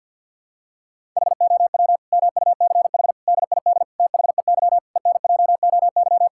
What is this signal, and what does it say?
Morse code, heard as a keyed tone, about 48 wpm, 700 Hz. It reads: HOW MUCH DID THEY ENJOY
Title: Sentences from Top 1000 Words - Review Part 5 - 50wpm